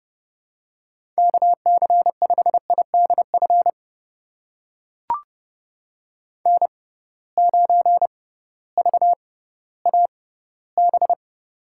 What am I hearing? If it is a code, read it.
Morse code, heard as a keyed tone, about 30 wpm, spaced as for 16 wpm, 700 Hz. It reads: KC5IDF N9VAB
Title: USA Call Signs - 6x Character Spacing - 30wpm